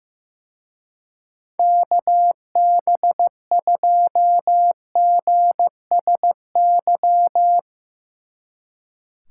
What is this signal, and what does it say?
Morse code, heard as a keyed tone, about 17 wpm, 700 Hz. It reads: KB2GSY
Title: Speed Racing - USA 2x3 Call Signs 15wpm